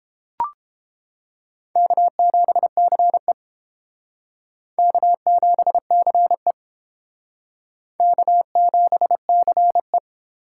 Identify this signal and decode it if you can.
Morse code, heard as a keyed tone, about 30 wpm, 700 Hz. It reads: K7CE K7CE K7CE